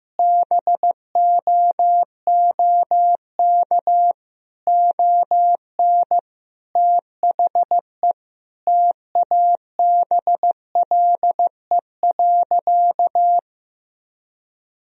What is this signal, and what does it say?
Morse code, heard as a keyed tone, about 15 wpm, 700 Hz. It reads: BOOK ON THE TABLE.
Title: Sentences from Top 1000 Words - Review Part 2 - 15wpm